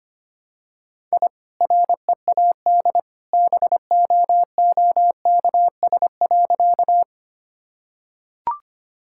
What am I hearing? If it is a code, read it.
Morse code, heard as a keyed tone, about 25 wpm, 700 Hz. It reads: I READ BOOKS.
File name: Easy Sentences with Hints - 25wpm